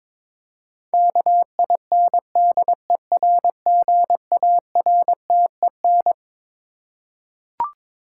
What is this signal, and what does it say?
Morse code, heard as a keyed tone, about 22 wpm, 700 Hz. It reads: KINDERGARTEN
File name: Twelve Letter Words - 22wpm